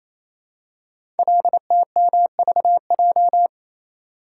Morse code, heard as a keyed tone, about 27 wpm, 700 Hz. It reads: LTMVJ
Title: Code Groups - 5 Letters - 28wpm